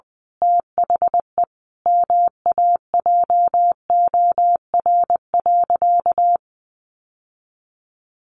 Morse code, heard as a keyed tone, about 20 wpm, 700 Hz. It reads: THE MAJOR.
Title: Sentences from Top 600 Words 20wpm